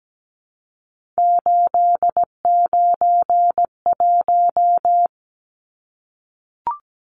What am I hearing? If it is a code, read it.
Morse code, heard as a keyed tone, about 17 wpm, 700 Hz. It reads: 891
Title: Numbers - 3-Digit 17wpm